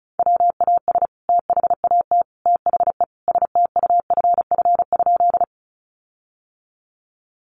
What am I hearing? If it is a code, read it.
Morse code, heard as a keyed tone, about 34 wpm, 700 Hz. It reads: WAS THAT THE STUFF?
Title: Sentences from Top 1400 Words 35wpm